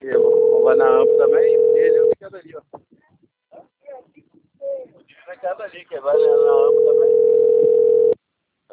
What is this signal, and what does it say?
Ringback tone (North America), 440 + 480 Hz, with speech.
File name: rg-1600-unknown- - - .28163